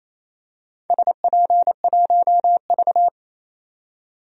Morse code, heard as a keyed tone, about 28 wpm, 700 Hz. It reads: SP1V
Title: DXCC-100 Call Signs - 4 Characters - 28wpm